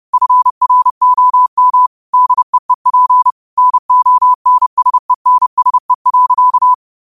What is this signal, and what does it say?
Morse code, heard as a keyed tone, about 30 wpm, 1000 Hz. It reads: FROM DEEP NONSENSE.
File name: qotdfiles / 30wpm